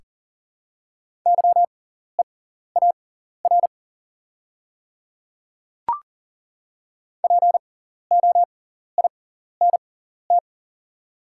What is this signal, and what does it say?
Morse code, heard as a keyed tone, about 40 wpm, spaced as for 14 wpm, 690 Hz. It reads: YEAR POINT
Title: 6x Character Spacing - Top 500 Words - 40wpm